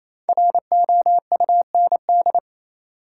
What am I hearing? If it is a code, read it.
Morse code, heard as a keyed tone, about 28 wpm, 700 Hz. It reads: ROUND
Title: Sets of 5 Words Encore - New Word Drill - 28wpm